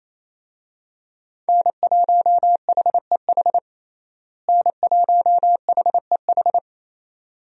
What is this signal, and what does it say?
Morse code, heard as a keyed tone, about 28 wpm, 700 Hz. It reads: N1HEH N1HEH
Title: USA Call Signs - 5 Characters - Repeated 2x - 28wpm